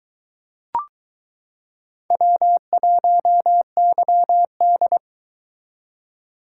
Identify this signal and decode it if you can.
Morse code, heard as a keyed tone, about 22 wpm, 700 Hz. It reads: W1YD